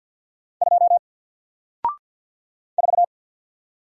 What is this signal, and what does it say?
Morse code, heard as a keyed tone, about 49 wpm, 700 Hz. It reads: 2 4
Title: Single Letter-Number Alt 50wpm